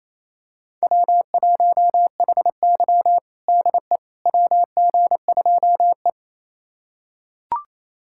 Morse code, heard as a keyed tone, about 28 wpm, 700 Hz. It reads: W1HY DE WG2E
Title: USA Callsign DE Callsign - 4 Characters - 28wpm